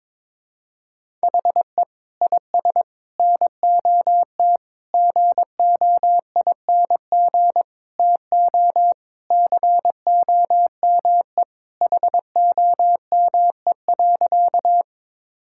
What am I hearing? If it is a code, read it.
Morse code, heard as a keyed tone, about 21 wpm, 700 Hz. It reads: HE IS NOT GOING TO COME HOME.
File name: practice / 22wpm / Sentences with 4-Word Phrases - 22wpm